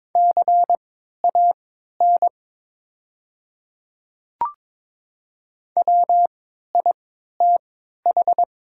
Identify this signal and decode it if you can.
Morse code, heard as a keyed tone, about 22 wpm, spaced as for 12 wpm, 700 Hz. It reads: CAN WITH
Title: 3x Character Spacing - Top 100 Words - 22wpm